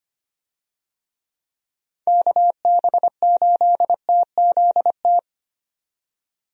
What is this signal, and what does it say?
Morse code, heard as a keyed tone, about 25 wpm, 700 Hz. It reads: KB8TZT
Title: USA Call Signs - 6 Characters - 25wpm